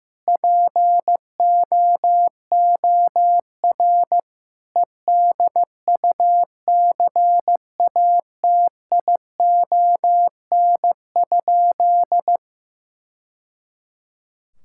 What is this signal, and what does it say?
Morse code, heard as a keyed tone, about 15 wpm, 700 Hz. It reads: POOR EDUCATION?
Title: Sentences from Top 1000 Words - Review Part 6 - 15wpm